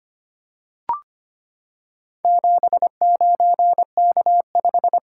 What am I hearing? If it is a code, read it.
Morse code, heard as a keyed tone, about 25 wpm, 700 Hz. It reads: 79K5
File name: practice / 25wpm / Four Letter-Number Combinations - 25wpm